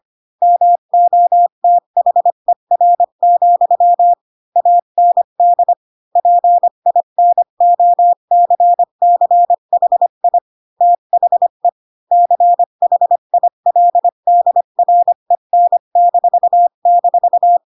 Morse code, heard as a keyed tone, about 25 wpm, 700 Hz. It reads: MOTHER, AND PINOCCHI THE CHILDREN--
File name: Pinocchio / Pinocchio ch3 20 25 ik1hge